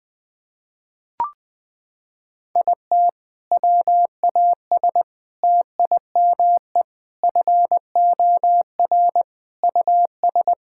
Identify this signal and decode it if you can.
Morse code, heard as a keyed tone, about 20 wpm, 700 Hz. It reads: IT WAS TIME FOR US
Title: Sentences from Top 1600 Words 20wpm